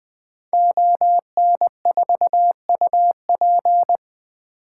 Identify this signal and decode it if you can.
Morse code, heard as a keyed tone, about 20 wpm, 700 Hz. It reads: ON4UP
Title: 2023 SOTA Activator Call Signs - 20wpm